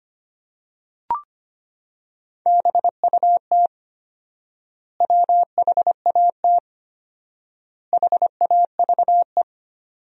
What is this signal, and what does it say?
Morse code, heard as a keyed tone, about 25 wpm, 700 Hz. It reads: BUT WHAT HAVE
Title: 4x Word-Spacing - Sentences from Top 100 Words - 25wpm